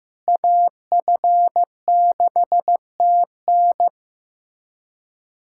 Morse code, heard as a keyed tone, about 15 wpm, 700 Hz. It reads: AF6TN
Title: USA Callsign DE Callsign - 5 Characters - 15wpm